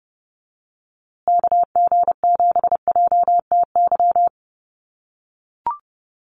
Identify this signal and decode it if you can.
Morse code, heard as a keyed tone, about 30 wpm, 700 Hz. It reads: KG7JTY
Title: USA Call Signs Repeated 2x 30wpm